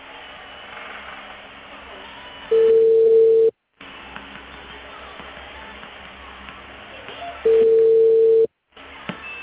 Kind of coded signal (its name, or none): telephone call progress tone